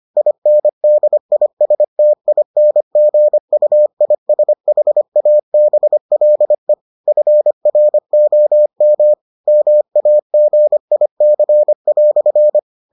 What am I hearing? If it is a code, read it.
Morse code, heard as a keyed tone, about 25 wpm, 600 Hz. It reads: INDISTINGUISHABLE FROM MAGIC"